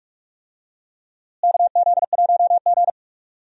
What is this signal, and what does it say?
Morse code, heard as a keyed tone, about 46 wpm, 700 Hz. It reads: KZ1G
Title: USA Call Signs - 4 Characters - Repeated 3x - 45wpm